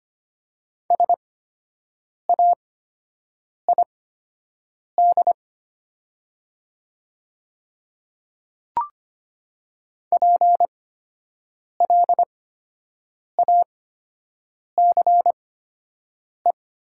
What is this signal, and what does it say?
Morse code, heard as a keyed tone, about 25 wpm, spaced as for 7 wpm, 700 Hz. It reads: SAID PLACE
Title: 8x Character Spacing - Top 200 Words - 25wpm